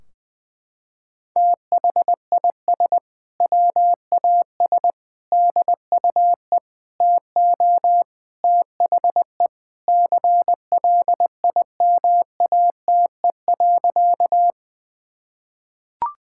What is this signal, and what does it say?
Morse code, heard as a keyed tone, about 20 wpm, 700 Hz. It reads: THIS WAS DUE TO THE CLIMATE.
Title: Sentences from Top 1600 Words 20wpm